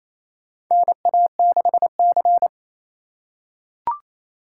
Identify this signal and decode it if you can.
Morse code, heard as a keyed tone, about 28 wpm, 700 Hz. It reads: NA6C
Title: USA Call Signs - 4 Characters - 4x Character Spacing - 28wpm